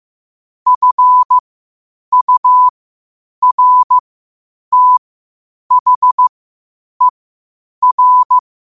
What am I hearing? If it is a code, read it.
Morse code, heard as a keyed tone, about 15 wpm, spaced as for 8 wpm, 1000 Hz. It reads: FURTHER
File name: qotdfiles / 8wpm